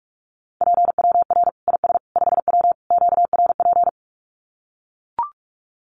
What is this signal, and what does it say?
Morse code, heard as a keyed tone, about 44 wpm, 700 Hz. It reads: PWR IS 5W QRP
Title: Common QSO Phrases 45wpm